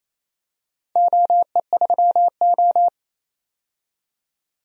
Morse code, heard as a keyed tone, about 28 wpm, 700 Hz. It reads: OE3O
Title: DXCC-100 Call Signs - 4 Characters - 28wpm